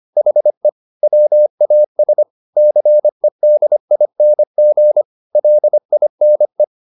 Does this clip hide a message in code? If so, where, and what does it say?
Yes: Morse code from 0.2 to 6.6 s, heard as a keyed tone, about 25 wpm, 600 Hz. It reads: HE WAS CEDING LINE